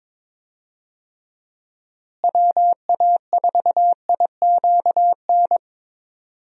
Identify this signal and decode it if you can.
Morse code, heard as a keyed tone, about 22 wpm, 700 Hz. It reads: WA4IQN
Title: USA Call Signs - 6 Characters - Repeated 3x - 22wpm